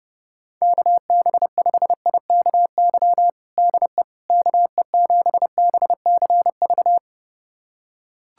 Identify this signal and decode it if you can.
Morse code, heard as a keyed tone, about 29 wpm, 700 Hz. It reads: KB5IKY DE KE7BCV
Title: USA Callsign DE Callsign - 6 Characters - 30wpm